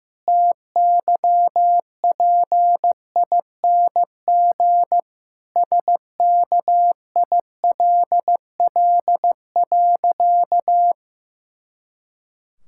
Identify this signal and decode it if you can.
Morse code, heard as a keyed tone, about 15 wpm, 700 Hz. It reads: TYPING SKILL.
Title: Pangram Challenge v2 - 15wpm